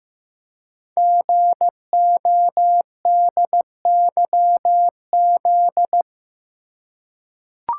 Morse code, heard as a keyed tone, about 15 wpm, 690 Hz. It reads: GODYZ
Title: Code Groups - 5 Letters - 15wpm